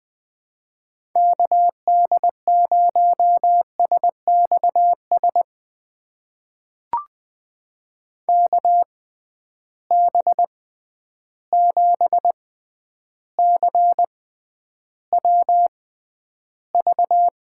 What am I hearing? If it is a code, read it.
Morse code, heard as a keyed tone, about 20 wpm, spaced as for 10 wpm, 700 Hz. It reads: KD0SXS KB7CWV
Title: USA Call Signs - 6 Characters - 6x Character Spacing - 20wpm